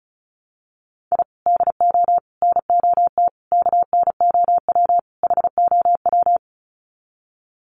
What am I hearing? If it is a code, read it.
Morse code, heard as a keyed tone, about 34 wpm, 700 Hz. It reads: I DO NOT KNOW HOW
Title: Sets of 5 Words 35wpm